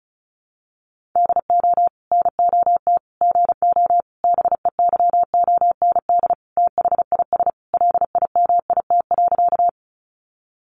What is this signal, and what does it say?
Morse code, heard as a keyed tone, about 34 wpm, 700 Hz. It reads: DO NOT GO BEYOND THIS LIMIT.
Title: Sentences from Top 1500 Words 35wpm